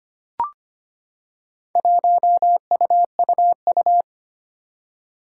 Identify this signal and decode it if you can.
Morse code, heard as a keyed tone, about 25 wpm, 700 Hz. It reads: 1UUU
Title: Four Challenging Letter-Number Combinations 25wpm